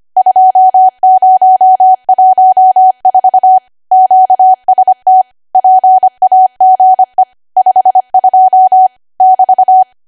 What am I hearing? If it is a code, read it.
Morse code, heard as a keyed tone, about 25 wpm, 750 Hz. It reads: 2014 QST PAGE 52 <BT>
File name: Archive / 25 WPM / 25WPM